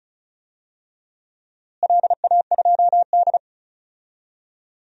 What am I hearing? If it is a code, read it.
Morse code, heard as a keyed tone, about 36 wpm, 700 Hz. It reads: LA2D
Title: DXCC-100 Call Signs - 4 Characters - Repeated 3x - 35wpm